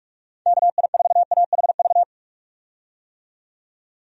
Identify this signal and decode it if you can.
Morse code, heard as a keyed tone, about 43 wpm, 700 Hz. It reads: KI4AHV
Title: USA Call Signs - 6 Characters - 45wpm